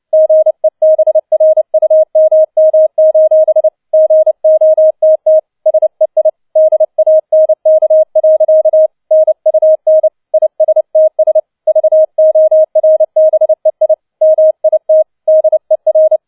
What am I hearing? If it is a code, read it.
Morse code, heard as a keyed tone, about 29 wpm, 630 Hz. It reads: GEBRUMM: GOTT SEI DANK. NUN ISTS VORBEI MIT DER
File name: 007 Letzter Streich 30wpm